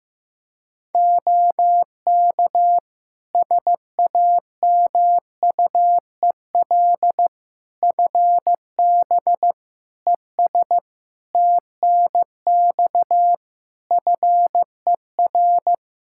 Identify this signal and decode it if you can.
Morse code, heard as a keyed tone, about 15 wpm, 700 Hz. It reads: OK SAMUEL FB ES TNX FER
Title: K1USN SST QSO Practice 15wpm